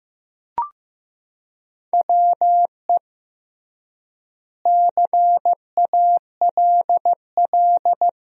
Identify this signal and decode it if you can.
Morse code, heard as a keyed tone, about 15 wpm, 700 Hz. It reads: WE CALL
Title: 3x Word-Spacing - Sentences from Top 400 Words - 15wpm